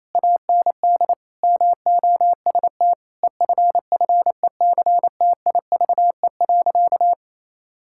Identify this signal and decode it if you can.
Morse code, heard as a keyed tone, about 28 wpm, 700 Hz. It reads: AND MOST EFFECTIVE.
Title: Sentences from Top 1000 Words - Review Part 6 - 28wpm